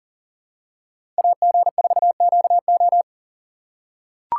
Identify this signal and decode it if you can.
Morse code, heard as a keyed tone, about 40 wpm, 700 Hz. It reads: AG4QO
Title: USA Call Signs - 5 Characters - 3x Character Spacing - 40wpm